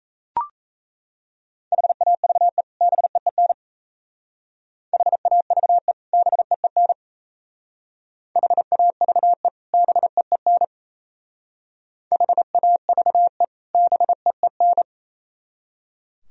Morse code, heard as a keyed tone, about 33 wpm, 700 Hz. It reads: HAVE BEEN HAVE BEEN HAVE BEEN HAVE BEEN
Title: Speed-Racing - Sets of 2 Words 28wpm